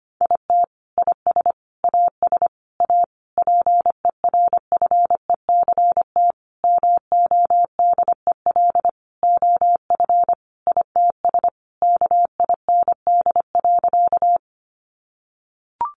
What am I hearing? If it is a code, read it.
Morse code, heard as a keyed tone, about 25 wpm, 700 Hz. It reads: IT IS AS A PERFECT MODEL OF ITS KIND.
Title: Sentences from Top 1000 Words 25wpm